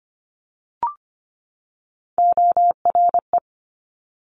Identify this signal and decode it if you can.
Morse code, heard as a keyed tone, about 25 wpm, 700 Hz. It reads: ORE
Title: Multi-letter Phonemes 25wpm